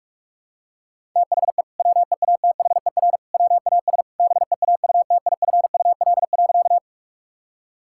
Morse code, heard as a keyed tone, about 41 wpm, 700 Hz. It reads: THE WEATHER WAS BEAUTIFUL.
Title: Sentences from Top 1000 Words - Review Part 5 - 45wpm